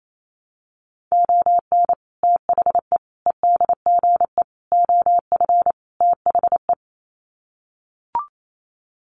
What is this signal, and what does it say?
Morse code, heard as a keyed tone, about 28 wpm, 700 Hz. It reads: ON THE EDGE OF THE
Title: Sets of 5 Words 28wpm